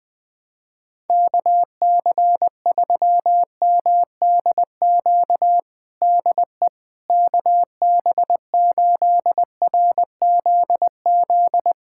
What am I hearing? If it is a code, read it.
Morse code, heard as a keyed tone, about 20 wpm, 700 Hz. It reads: KC3MDQ DE KB8RZZ
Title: USA Callsign DE Callsign 20wpm